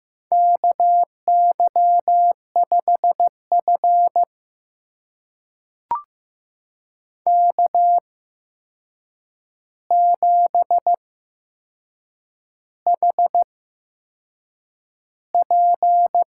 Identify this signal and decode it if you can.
Morse code, heard as a keyed tone, about 15 wpm, spaced as for 6 wpm, 700 Hz. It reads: KY5F K7HP
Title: USA Call Signs - 4 Characters - 8x Character Spacing - 15wpm